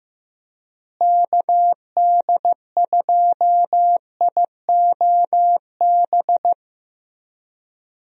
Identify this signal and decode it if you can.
Morse code, heard as a keyed tone, about 15 wpm, 700 Hz. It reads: KD2IOB
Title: USA Call Signs - 6 Characters - Repeated 3x - 15wpm